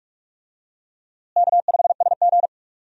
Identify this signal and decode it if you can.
Morse code, heard as a keyed tone, about 46 wpm, 700 Hz. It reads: K5SG
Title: USA Call Signs - 4 Characters - Repeated 3x - 45wpm